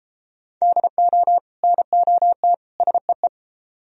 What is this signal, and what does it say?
Morse code, heard as a keyed tone, about 30 wpm, 700 Hz. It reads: DO NOT SEE